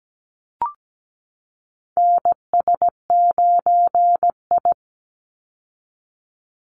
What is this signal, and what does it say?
Morse code, heard as a keyed tone, about 17 wpm, 700 Hz. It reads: NS9I
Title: Call Signs Repeated 2x - 17wpm